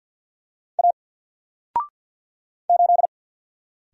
Morse code, heard as a keyed tone, about 48 wpm, 700 Hz. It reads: A 8
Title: Single Letter-Number Alt 50wpm